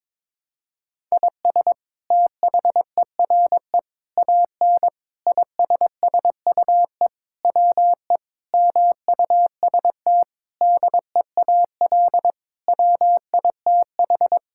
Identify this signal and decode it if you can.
Morse code, heard as a keyed tone, about 22 wpm, 700 Hz. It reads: IS THERE AN ISSUE WE MUST DEAL WITH
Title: Sentences from Top 1000 Words - Review Part 6 - 22wpm